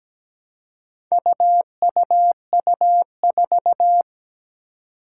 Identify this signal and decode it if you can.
Morse code, heard as a keyed tone, about 17 wpm, 700 Hz. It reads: UUU4
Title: Four Challenging Letter-Number Combinations 17wpm